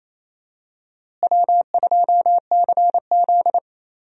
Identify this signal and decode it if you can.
Morse code, heard as a keyed tone, about 28 wpm, 700 Hz. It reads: W2CZ
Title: USA Call Signs - 4 Characters - Repeated 2x - 28wpm